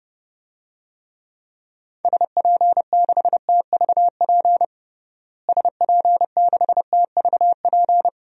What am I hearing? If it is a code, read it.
Morse code, heard as a keyed tone, about 30 wpm, 700 Hz. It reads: SP6TVP SP6TVP
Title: DXCC-100 Call Signs - 6 Characters - Repeated 2x - 30wpm